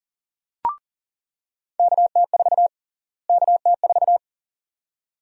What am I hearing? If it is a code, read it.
Morse code, heard as a keyed tone, about 40 wpm, 700 Hz. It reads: KT4 KT4
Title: USA Call Sign Prefixes - Repeated 2x - 40wpm